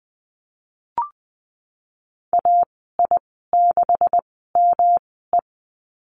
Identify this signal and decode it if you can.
Morse code, heard as a keyed tone, about 20 wpm, spaced as for 15 wpm, 700 Hz. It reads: AI6ME
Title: 128 Lesson 21g - Number 6 - Call Signs 20wpm